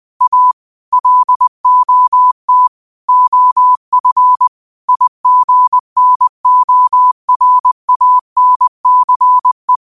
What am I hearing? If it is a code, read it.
Morse code, heard as a keyed tone, about 21 wpm, 1000 Hz. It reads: A LOT OF IGNORANCE
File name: qotdfiles / 20wpm